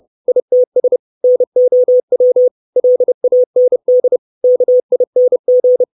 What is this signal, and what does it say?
Morse code, heard as a keyed tone, about 30 wpm, 500 Hz. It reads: ITS NOW LAND KING
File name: 632 Words 30WPM